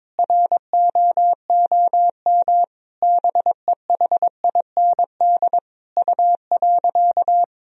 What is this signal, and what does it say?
Morse code, heard as a keyed tone, about 22 wpm, 700 Hz. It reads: ROOM BEHIND U.
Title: Sentences from Top 500 Words and Abbreviations 22wpm